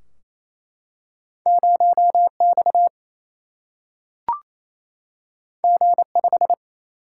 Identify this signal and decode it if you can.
Morse code, heard as a keyed tone, about 28 wpm, 700 Hz. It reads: 0 X G 5
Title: Single-Letter Single-Number 28wpm